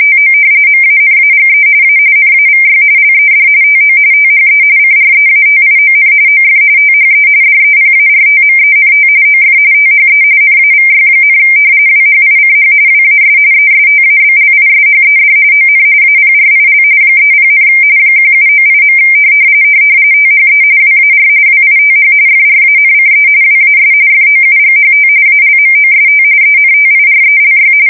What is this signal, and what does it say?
RTTY, about 45.45 baud. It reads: TO MANAGE A NAVY LED TO THE CREATION OF THE DEPARTMENT OF THE NAVY, WHICH WAS ESTABLISHED ON APRIL 30, 1798. THE WAR WITH FRANCE WAS FOUGHT ALMOST ENTIRELY
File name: fsk170 / usnavy1